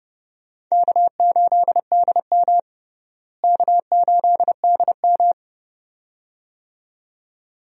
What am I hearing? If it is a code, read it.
Morse code, heard as a keyed tone, about 32 wpm, 700 Hz. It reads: K8DM K8DM
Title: USA Call Signs - 4 Characters - Repeated 2x - 30wpm